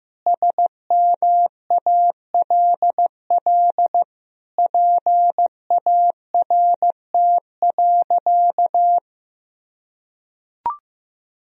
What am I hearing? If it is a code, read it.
Morse code, heard as a keyed tone, about 15 wpm, 700 Hz. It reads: SMALL PART.
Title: Sentences from Top 1000 Words - Review Part 1 - 15wpm